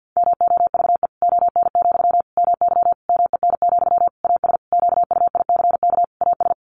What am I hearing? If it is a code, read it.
Morse code, heard as a keyed tone, about 50 wpm, 700 Hz. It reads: MOVE ON, MY MEN, AS QUICK AS
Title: Sentences from Top 500 Words 50wpm